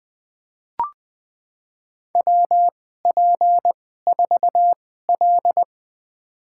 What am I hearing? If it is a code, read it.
Morse code, heard as a keyed tone, about 21 wpm, spaced as for 15 wpm, 700 Hz. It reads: WP4L
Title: USA Call Signs - 4 Characters - 2x Character Spacing - 20wpm